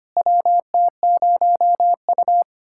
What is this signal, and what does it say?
Morse code, heard as a keyed tone, about 25 wpm, 700 Hz. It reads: WT0U
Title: USA Callsign DE Callsign - 4 Characters - 25wpm